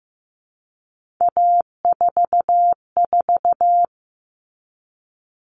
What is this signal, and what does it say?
Morse code, heard as a keyed tone, about 15 wpm, 700 Hz. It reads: A44
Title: All Cut Numbers - 3-Digit 15wpm